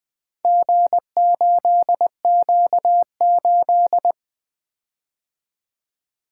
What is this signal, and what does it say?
Morse code, heard as a keyed tone, about 20 wpm, 700 Hz. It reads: G8Q8
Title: Four Letter-Number Combinations - 20wpm